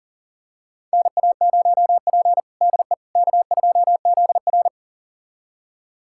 Morse code, heard as a keyed tone, about 40 wpm, 700 Hz. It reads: NA0P DE K2ZR
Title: USA Callsign DE Callsign - 4 Characters - 40wpm